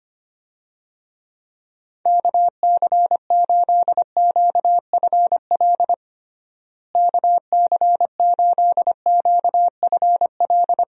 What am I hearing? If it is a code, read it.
Morse code, heard as a keyed tone, about 25 wpm, 700 Hz. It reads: KC8QFL KC8QFL
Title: USA Call Signs - 6 Characters - Repeated 3x - 25wpm